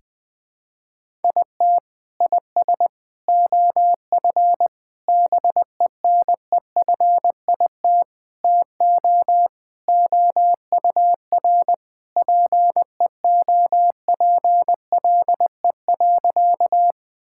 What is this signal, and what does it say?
Morse code, heard as a keyed tone, about 20 wpm, 700 Hz. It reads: IT IS OF BENEFIT TO OUR PEOPLE.
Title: Sentences from Top 1000 Words - Review Part 6 - 20wpm